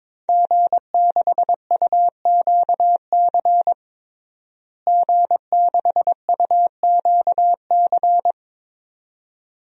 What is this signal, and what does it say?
Morse code, heard as a keyed tone, about 22 wpm, 700 Hz. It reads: G6UQC G6UQC
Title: DXCC-100 Call Signs - 5 Characters - Repeated 2x - 22wpm